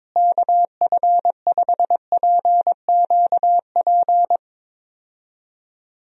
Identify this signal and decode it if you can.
Morse code, heard as a keyed tone, about 22 wpm, 700 Hz. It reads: KF5PQP
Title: USA Call Signs - 6 Characters - Repeated 2x - 22wpm